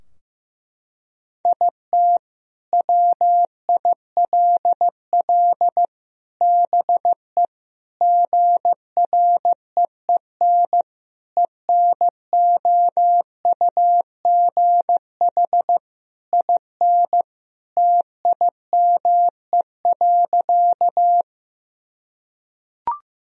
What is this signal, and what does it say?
Morse code, heard as a keyed tone, about 16 wpm, 700 Hz. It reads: IT WILL BE GREEN ENOUGH IN TIME.
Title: Sentences from Top 1000 Words - Review Part 2 - 15wpm